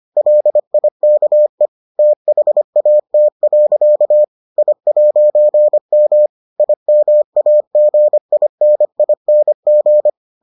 Morse code, heard as a keyed tone, about 25 wpm, 600 Hz. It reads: LIKE THAT. I'M IMAGINING